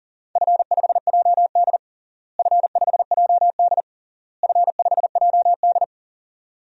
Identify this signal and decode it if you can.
Morse code, heard as a keyed tone, about 37 wpm, 700 Hz. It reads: F5JD F5JD F5JD
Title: DXCC-100 Call Signs - 4 Characters - Repeated 3x - 40wpm